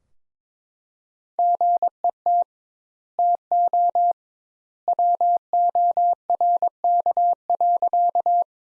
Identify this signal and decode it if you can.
Morse code, heard as a keyed tone, about 22 wpm, 700 Hz. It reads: GET TO WORK.
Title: Sentences Top 200 Words - 22wpm